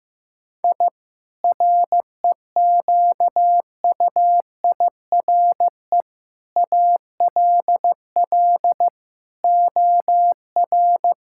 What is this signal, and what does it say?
Morse code, heard as a keyed tone, about 15 wpm, 700 Hz. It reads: I REQUIRE ALL OR
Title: Sentences from Top 1000 Words - Review Part 6 - 15wpm